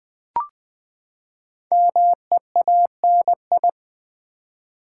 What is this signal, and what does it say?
Morse code, heard as a keyed tone, about 20 wpm, 700 Hz. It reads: MEANI
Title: Build A Word - 5 Letters At A Time - 10-Letter Words - 20wpm